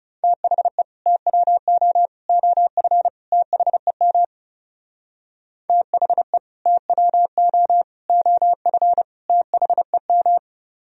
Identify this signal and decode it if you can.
Morse code, heard as a keyed tone, about 40 wpm, 700 Hz. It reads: THE TWO OF THEM THE TWO OF THEM